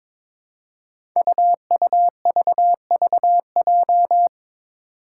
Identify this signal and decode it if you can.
Morse code, heard as a keyed tone, about 21 wpm, 700 Hz. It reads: UUVVJ
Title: Five Challenging Letter-Number Combinations 22wpm